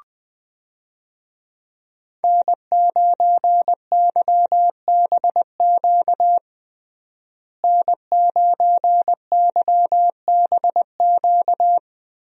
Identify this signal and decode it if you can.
Morse code, heard as a keyed tone, about 21 wpm, 700 Hz. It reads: N9YBQ N9YBQ
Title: USA Call Signs - 5 Characters - Repeated 3x - 20wpm